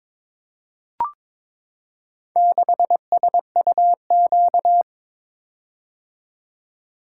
Morse code, heard as a keyed tone, about 22 wpm, 700 Hz. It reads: 6SUQ
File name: practice / 22wpm / Four Letter-Number Combinations - 22wpm